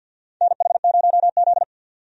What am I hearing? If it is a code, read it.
Morse code, heard as a keyed tone, about 47 wpm, 700 Hz. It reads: NH0Z
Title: USA Call Signs - 4 Characters - 50wpm